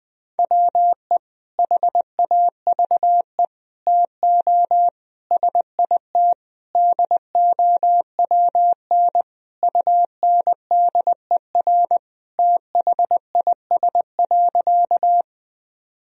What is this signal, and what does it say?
Morse code, heard as a keyed tone, about 20 wpm, 700 Hz. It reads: WE HAVE TO SIT DOWN UNDER THIS.
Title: Sentences from Top 1000 Words - Review Part 1 - 20wpm